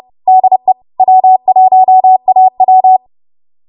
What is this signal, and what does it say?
Morse code, heard as a keyed tone, about 30 wpm, 750 Hz. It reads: DE W1AW
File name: Archive / 30 WPM / 30WPM